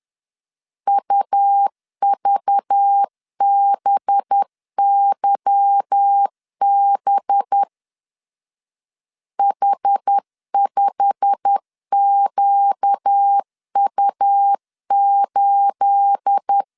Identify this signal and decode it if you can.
Morse code, heard as a keyed tone, about 10 wpm, 800 Hz. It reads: UVBYB H5QU8
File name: CW / DET / 1 9 1500C S B 10wpm 4439